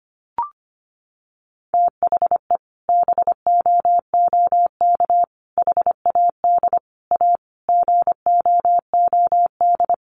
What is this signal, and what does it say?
Morse code, heard as a keyed tone, about 25 wpm, 700 Hz. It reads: THE BOOK HAD A GOOD
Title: Sentences from Top 1400 Words 25wpm